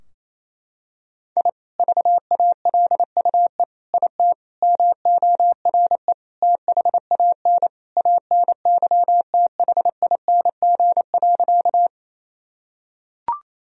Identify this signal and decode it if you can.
Morse code, heard as a keyed tone, about 27 wpm, 700 Hz. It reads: I VALUE IT MORE THAN ANYTHING.
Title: Sentences from Top 1000 Words - Review Part 3 - 28wpm